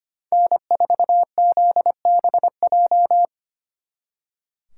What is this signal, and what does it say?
Morse code, heard as a keyed tone, about 30 wpm, 700 Hz. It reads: N4ZBJ
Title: Speed Racing - USA 1x3 Call Signs 25wpm